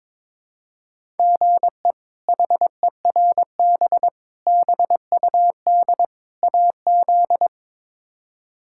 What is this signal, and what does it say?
Morse code, heard as a keyed tone, about 22 wpm, 700 Hz. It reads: GE HERB BUD AZ
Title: K1USN SST QSO Practice - 22wpm